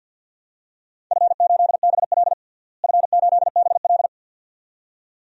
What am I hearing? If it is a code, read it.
Morse code, heard as a keyed tone, about 48 wpm, 700 Hz. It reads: F8BL F8BL
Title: DXCC-100 Call Signs - 4 Characters - Repeated 2x - 50wpm